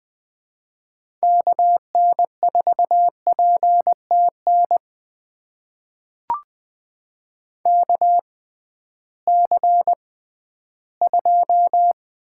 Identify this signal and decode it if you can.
Morse code, heard as a keyed tone, about 20 wpm, spaced as for 13 wpm, 700 Hz. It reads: KN4PTN KC2
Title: USA Call Signs - 6 Characters - 6x Character Spacing - 20wpm